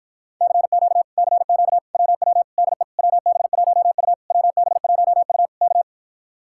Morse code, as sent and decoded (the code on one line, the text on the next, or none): -.-. --.- / -.-. --.- / .-- .-- / -.. / .-- -... .---- ..- / .-- -... .---- ..- / -.-
CQ CQ WW DE WB1U WB1U K